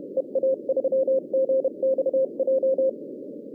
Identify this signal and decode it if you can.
Morse code, heard as a keyed tone, about 29 wpm, 590 Hz, with noise. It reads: EA3GXJ
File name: 2011 bcc-challenge samstag tape